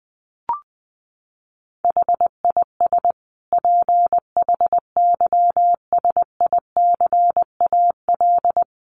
Morse code, heard as a keyed tone, about 19 wpm, 700 Hz. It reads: HIS PHYSICAL
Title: Sentences from Top 1400 Words 20wpm